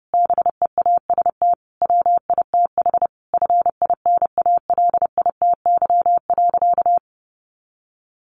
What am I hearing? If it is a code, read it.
Morse code, heard as a keyed tone, about 30 wpm, 700 Hz. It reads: BEAST WITH FINALITY.
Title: A Sound Of Thunder 30wpm